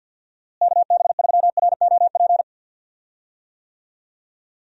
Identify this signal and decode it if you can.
Morse code, heard as a keyed tone, about 52 wpm, 700 Hz. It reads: KB3ROP
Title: USA Call Signs - 6 Characters - Repeated 3x - 50wpm